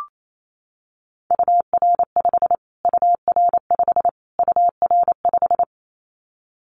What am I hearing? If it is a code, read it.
Morse code, heard as a keyed tone, about 28 wpm, 700 Hz. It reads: UR5 UR5 UR5
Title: Callsign Prefixes for Top 25 DXCC - Repeated 3x - 28wpm